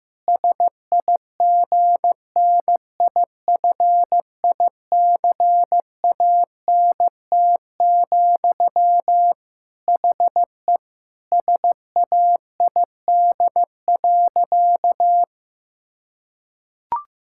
Morse code, heard as a keyed tone, about 15 wpm, 700 Hz. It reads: SIGNIFICANT, HE SAID.
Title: Sentences from Top 1000 Words - Review Part 6 - 15wpm